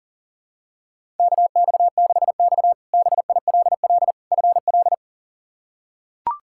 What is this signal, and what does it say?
Morse code, heard as a keyed tone, about 40 wpm, 690 Hz. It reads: KX6X BILL FL
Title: K1USN SST QSO Practice - 40wpm